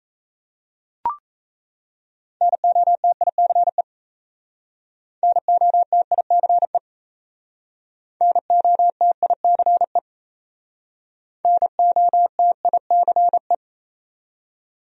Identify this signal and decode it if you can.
Morse code, heard as a keyed tone, about 37 wpm, 700 Hz. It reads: NOTICE NOTICE NOTICE NOTICE